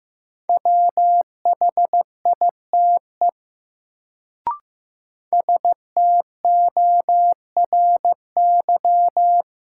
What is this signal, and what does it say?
Morse code, heard as a keyed tone, about 15 wpm, 700 Hz. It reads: WHITE STORY
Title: Sentences from Top 300 Words - New Word Drill - 15wpm